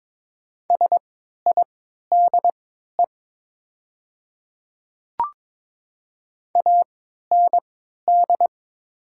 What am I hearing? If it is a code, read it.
Morse code, heard as a keyed tone, about 22 wpm, spaced as for 12 wpm, 700 Hz. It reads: SIDE AND
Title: 3x Character Spacing - Top 100 Words - 22wpm